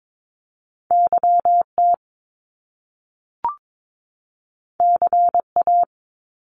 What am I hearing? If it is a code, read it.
Morse code, heard as a keyed tone, about 22 wpm, 700 Hz. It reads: YT CA
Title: Top Two Letter Combinations in Words 22wpm